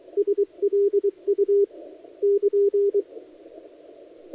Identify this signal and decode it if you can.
Morse code, heard as a keyed tone, about 23 wpm, 400 Hz. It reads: SLU <KN>
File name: dlazaawansowanych / nagraniazpasma / titawaNETcontrol